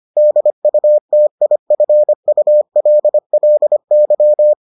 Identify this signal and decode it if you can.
Morse code, heard as a keyed tone, about 25 wpm, 600 Hz. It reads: DUTIFULLY